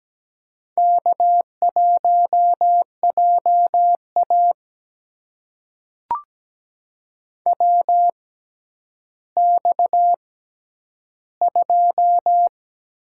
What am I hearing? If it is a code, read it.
Morse code, heard as a keyed tone, about 17 wpm, spaced as for 10 wpm, 700 Hz. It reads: K1JA WX2
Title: USA Call Signs - 4 Characters - 6x Character Spacing - 17wpm